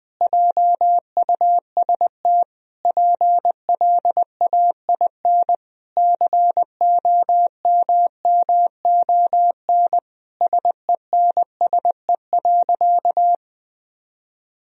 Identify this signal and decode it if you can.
Morse code, heard as a keyed tone, about 20 wpm, 700 Hz. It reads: JUST PLAIN COMMON SENSE.
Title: Sentences from Top 1000 Words - Review Part 4 - 20wpm